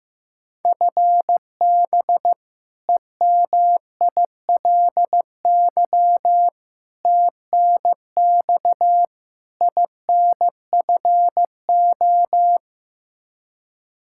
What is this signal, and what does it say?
Morse code, heard as a keyed tone, about 15 wpm, 700 Hz. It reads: FB EMILY TNX INFO
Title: Practice QSO - Exchange 3 - More Info - 15wpm